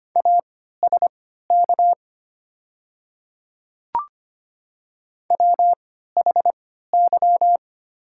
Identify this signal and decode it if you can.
Morse code, heard as a keyed tone, about 25 wpm, spaced as for 14 wpm, 700 Hz. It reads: ASK WHY
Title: 3x Character Spacing - Top 200 Words - 25wpm